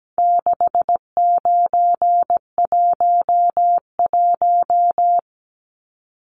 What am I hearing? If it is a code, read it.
Morse code, heard as a keyed tone, about 17 wpm, 700 Hz. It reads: 6911
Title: Numbers - 4-Digit 17wpm